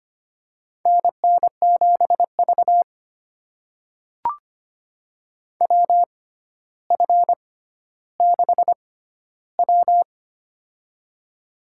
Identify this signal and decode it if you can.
Morse code, heard as a keyed tone, about 25 wpm, spaced as for 13 wpm, 700 Hz. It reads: NN7V WF6W
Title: USA Call Signs - 4 Characters - 6x Character Spacing - 25wpm